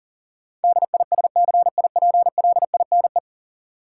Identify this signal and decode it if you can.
Morse code, heard as a keyed tone, about 40 wpm, 700 Hz. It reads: DISCIPLINE
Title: Sentences from Top 900 Words - New Word Drill - 40wpm